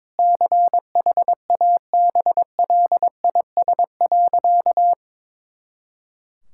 Morse code, heard as a keyed tone, about 22 wpm, 700 Hz. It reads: CHABLIS.
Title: Pangram Challenge v2 - 22wpm